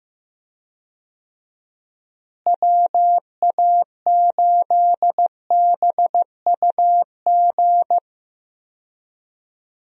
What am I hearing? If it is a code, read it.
Morse code, heard as a keyed tone, about 15 wpm, 700 Hz. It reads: WA8BUG
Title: USA Call Signs - 6 Characters - Repeated 2x - 15wpm